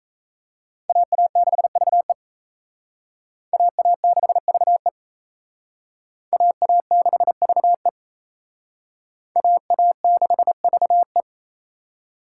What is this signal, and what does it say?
Morse code, heard as a keyed tone, about 35 wpm, 700 Hz. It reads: AA6VE AA6VE AA6VE AA6VE
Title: Speed Racing - USA Call Signs 28wpm